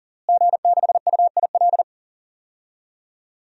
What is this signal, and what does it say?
Morse code, heard as a keyed tone, about 40 wpm, 700 Hz. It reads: G6UIL
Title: DXCC-100 Call Signs - 5 Characters - Repeated 3x - 40wpm